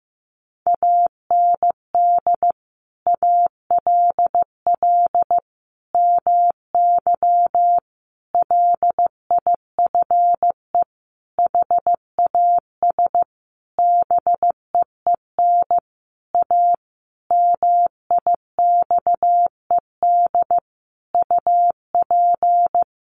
Morse code, heard as a keyed tone, about 15 wpm, 700 Hz. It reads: AND ALL MY LIFE HAS BEEN A MIXED UP
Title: Sentences from Top 1500 Words 15wpm